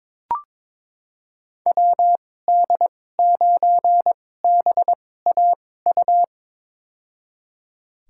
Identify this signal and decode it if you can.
Morse code, heard as a keyed tone, about 21 wpm, spaced as for 16 wpm, 700 Hz. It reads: WD9BAU
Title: USA Call Signs - 6 Characters - 2x Character Spacing - 22wpm